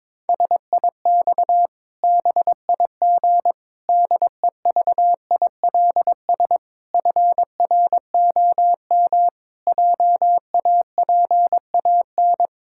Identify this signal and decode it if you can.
Morse code, heard as a keyed tone, about 22 wpm, 700 Hz. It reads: SIX BIG DEVILS FROM JAPAN
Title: Pangram Challenge v2 - 22wpm